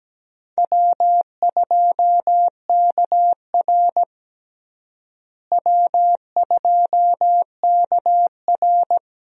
Morse code, heard as a keyed tone, about 17 wpm, 700 Hz. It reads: W2KR W2KR
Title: USA Call Signs - 4 Characters - Repeated 2x - 17wpm